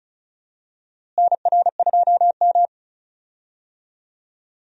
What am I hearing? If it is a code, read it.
Morse code, heard as a keyed tone, about 36 wpm, 700 Hz. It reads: NR2M
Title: USA Call Signs - 4 Characters - Repeated 3x - 35wpm